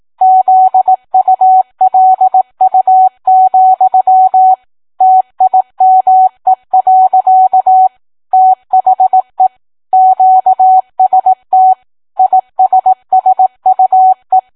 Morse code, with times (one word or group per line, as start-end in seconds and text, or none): 0.2-4.5 ZULU,
5.0-7.9 TIME.
8.3-9.5 THE
9.9-11.7 QST
12.2-14.4 ISSUE